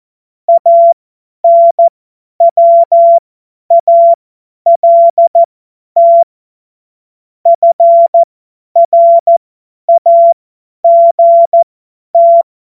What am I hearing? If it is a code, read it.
Morse code, heard as a keyed tone, about 14 wpm, spaced as for 10 wpm, 680 Hz. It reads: ANWALT FRAGT
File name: Morsekurs / Witze / Witz 01. November 2022 50 F70